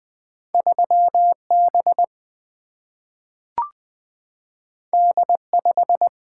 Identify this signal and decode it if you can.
Morse code, heard as a keyed tone, about 20 wpm, 700 Hz. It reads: 3B D5
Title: Single-Letter Single-Number 20wpm